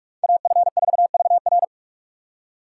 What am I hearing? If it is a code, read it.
Morse code, heard as a keyed tone, about 44 wpm, 700 Hz. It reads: AF4VR